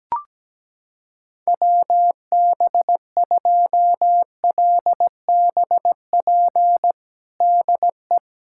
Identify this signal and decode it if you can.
Morse code, heard as a keyed tone, about 17 wpm, 700 Hz. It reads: WB2LBP DE
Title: USA Callsign DE Callsign - 6 Characters - 17wpm